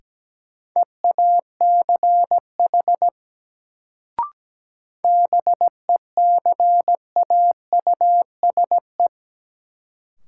Morse code, heard as a keyed tone, about 17 wpm, 700 Hz. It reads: EACH BECAUSE
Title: Sets of 2 Words - New Word Drill - 17wpm